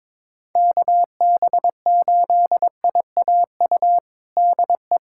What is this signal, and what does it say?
Morse code, heard as a keyed tone, about 22 wpm, 700 Hz. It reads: KB8IAU DE
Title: USA Callsign DE Callsign - 6 Characters - 22wpm